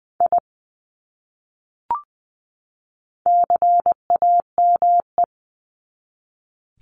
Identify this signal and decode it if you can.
Morse code, heard as a keyed tone, about 20 wpm, 700 Hz. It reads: I CAME
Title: Top 300 Words 20wpm